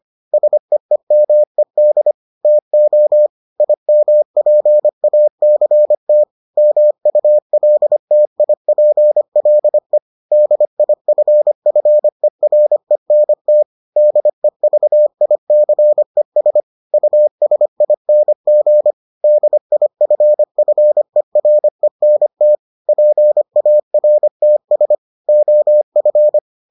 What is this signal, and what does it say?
Morse code, heard as a keyed tone, about 25 wpm, 600 Hz. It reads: SEEMED TO IMPACT MULTIPLE DIFFERENT DEVICES USING DIFFERENT PARTS OF